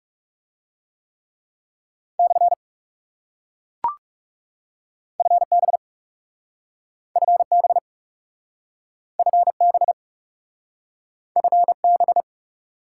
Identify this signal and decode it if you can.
Morse code, heard as a keyed tone, about 40 wpm, 700 Hz. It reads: / FB FB FB FB